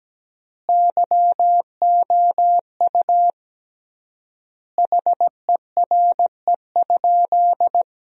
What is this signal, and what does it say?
Morse code, heard as a keyed tone, about 17 wpm, 700 Hz. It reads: YOU HERE?
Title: 3x Word-Spacing - Sentences from Top 200 Words - 17wpm